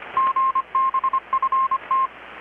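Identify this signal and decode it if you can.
Morse code, heard as a keyed tone, about 25 wpm, 1060 Hz. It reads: GBFT
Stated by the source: Morse from KPH's Night of Nights XX maritime radiotelegraphy event, received in Chiapas, Mexico about 3,800 km away on a Kenwood R-600 with a grounded 10 m long-wire antenna.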